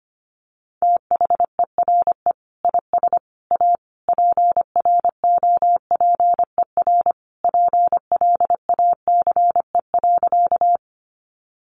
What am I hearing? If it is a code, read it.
Morse code, heard as a keyed tone, about 25 wpm, 700 Hz. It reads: THERE IS A PROPER PLACE.
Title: Sentences from Top 800 Words 25wpm